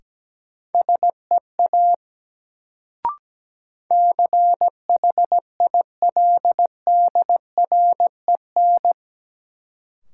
Morse code, heard as a keyed tone, about 17 wpm, 700 Hz. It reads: SEA CHILDREN
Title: Sentences from Top 300 Words - New Word Drill - 17wpm